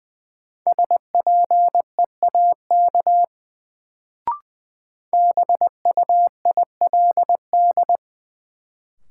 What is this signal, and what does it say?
Morse code, heard as a keyed tone, about 20 wpm, 700 Hz. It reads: SPEAK BUILD
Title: Sentences from Top 1100 Words - New Word Drill Alt - 20wpm